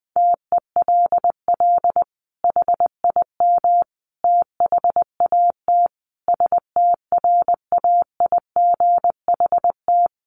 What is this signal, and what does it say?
Morse code, heard as a keyed tone, about 20 wpm, 700 Hz. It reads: TELL HIM THAT STRAIGHT
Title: Sentences from Top 1500 Words 20wpm